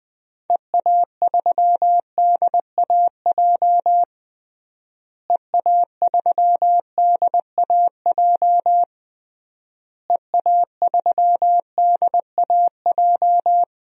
Morse code, heard as a keyed tone, about 20 wpm, 700 Hz. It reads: EA3DAJ EA3DAJ EA3DAJ
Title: DXCC-100 Call Signs - 6 Characters - Repeated 3x - 20wpm